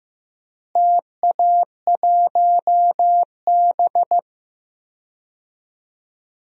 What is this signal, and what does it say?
Morse code, heard as a keyed tone, about 16 wpm, 700 Hz. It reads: TA1B
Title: DXCC-100 Call Signs - 4 Characters - Repeated 3x - 15wpm